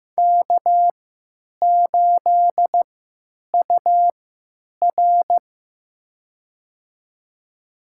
Morse code, heard as a keyed tone, about 16 wpm, spaced as for 9 wpm, 700 Hz. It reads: K8UR
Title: USA Call Signs - 4 Characters - 3x Character Spacing - 15wpm